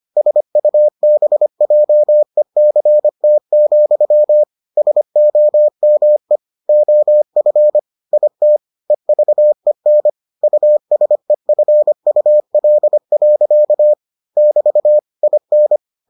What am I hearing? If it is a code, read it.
Morse code, heard as a keyed tone, about 25 wpm, 600 Hz. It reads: SUBJECT, SOME OF IT EVEN USEFUL. <BT> IN